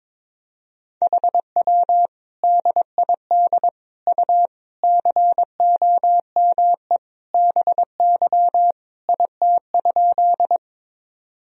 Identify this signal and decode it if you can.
Morse code, heard as a keyed tone, about 22 wpm, 700 Hz. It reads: HW DID U COME BY IT?
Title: Sentences from Top 100 Words and Abbreviations 22wpm